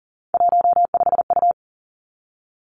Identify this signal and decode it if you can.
Morse code, heard as a keyed tone, about 39 wpm, 700 Hz. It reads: 15U
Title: Three Challenging Letter-Number Combinations 40wpm